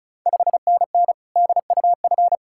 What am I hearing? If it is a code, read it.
Morse code, heard as a keyed tone, about 36 wpm, 700 Hz. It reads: 5NN DUF
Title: Ontario QSO Party Exchange 35wpm